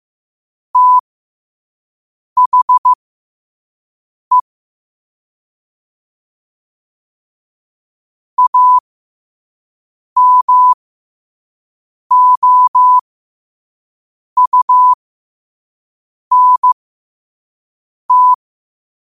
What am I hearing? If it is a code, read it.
Morse code, heard as a keyed tone, about 15 wpm, spaced as for 5 wpm, 1000 Hz. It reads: THE AMOUNT
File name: qotdfiles / 5wpm